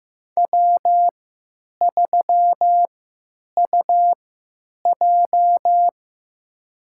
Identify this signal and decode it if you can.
Morse code, heard as a keyed tone, about 16 wpm, spaced as for 9 wpm, 700 Hz. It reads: W3UJ
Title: USA Call Signs - 4 Characters - 3x Character Spacing - 15wpm